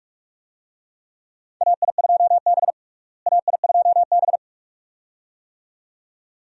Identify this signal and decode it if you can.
Morse code, heard as a keyed tone, about 42 wpm, 700 Hz. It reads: AI2B AI2B
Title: USA Call Signs - 4 Characters - Repeated 2x - 45wpm